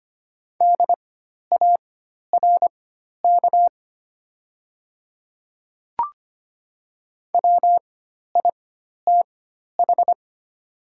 Morse code, heard as a keyed tone, about 25 wpm, spaced as for 12 wpm, 690 Hz. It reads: DARK WITH
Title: 4x Character Spacing - Top 500 Words - 25wpm